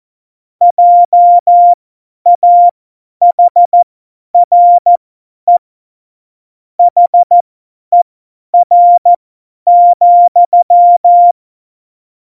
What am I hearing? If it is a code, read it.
Morse code, heard as a keyed tone, about 14 wpm, spaced as for 10 wpm, 700 Hz. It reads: JAHRE HER,
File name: Morsekurs / Witze / Witz 08. Dezember 2020, 50 F70